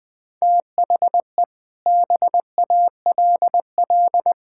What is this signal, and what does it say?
Morse code, heard as a keyed tone, about 20 wpm, 700 Hz. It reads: THE BALL
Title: Sentences from Top 1600 Words 20wpm